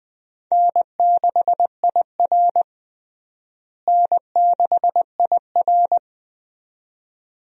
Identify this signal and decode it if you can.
Morse code, heard as a keyed tone, about 20 wpm, 700 Hz. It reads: N6IR N6IR
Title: USA Call Signs - 4 Characters - Repeated 2x - 20wpm